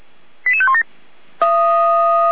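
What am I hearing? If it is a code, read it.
Five-tone selective call (ZVEI), heard as quick runs of five tones, tones of about 70 ms. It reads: 88517
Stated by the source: ZVEI five-tone selective call received on SDR.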